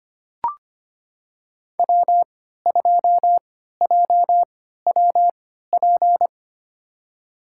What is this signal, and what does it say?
Morse code, heard as a keyed tone, about 25 wpm, spaced as for 14 wpm, 700 Hz. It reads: W2JWP
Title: USA Call Signs - 5 Characters - 3x Character Spacing - 25wpm